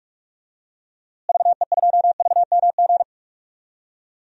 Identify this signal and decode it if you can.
Morse code, heard as a keyed tone, about 42 wpm, 700 Hz. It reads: VE2VMG
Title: DXCC-100 Call Signs - 6 Characters - Repeated 3x - 45wpm